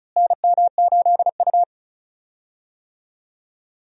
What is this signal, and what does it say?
Morse code, heard as a keyed tone, about 36 wpm, 700 Hz. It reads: NM8U
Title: USA Call Signs - 4 Characters - Repeated 2x - 35wpm